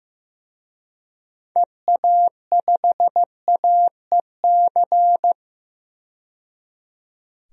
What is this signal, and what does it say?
Morse code, heard as a keyed tone, about 16 wpm, 690 Hz. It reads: EA5AEC
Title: DXCC-100 Call Signs - 6 Characters - 15wpm